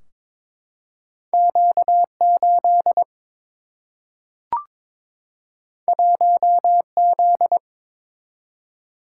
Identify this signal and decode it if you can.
Morse code, heard as a keyed tone, about 21 wpm, 700 Hz. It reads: Q 8 1 Z
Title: Single-Letter Single-Number 22wpm